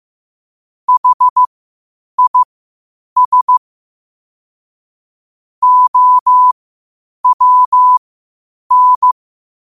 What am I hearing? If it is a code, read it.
Morse code, heard as a keyed tone, about 15 wpm, spaced as for 8 wpm, 1000 Hz. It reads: HIS OWN
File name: qotdfiles / 8wpm